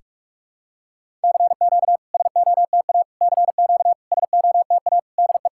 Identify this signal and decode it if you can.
Morse code, heard as a keyed tone, about 48 wpm, 700 Hz. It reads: CQ SOTA CQ SOTA DE
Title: SOTA QSO Practice 45wpm